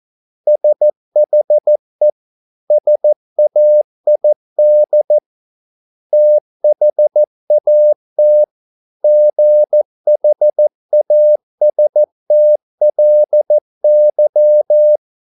Morse code, heard as a keyed tone, about 14 wpm, 600 Hz. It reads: SHE SAID THAT GHASTLY